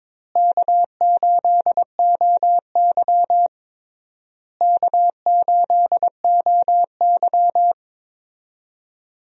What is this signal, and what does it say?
Morse code, heard as a keyed tone, about 22 wpm, 700 Hz. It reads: K8OY K8OY
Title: USA Call Signs - 4 Characters - Repeated 3x - 22wpm